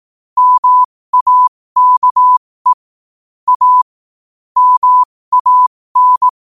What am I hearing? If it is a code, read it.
Morse code, heard as a keyed tone, about 18 wpm, spaced as for 15 wpm, 1000 Hz. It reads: MAKE A MAN
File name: qotdfiles / 15wpm